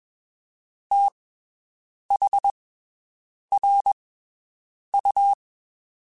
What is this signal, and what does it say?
Morse code, heard as a keyed tone, about 22 wpm, spaced as for 7 wpm, 800 Hz. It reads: THRU
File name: (spaced) / QSOs Spaced-mp3s / WD0FBT 1 @20 sp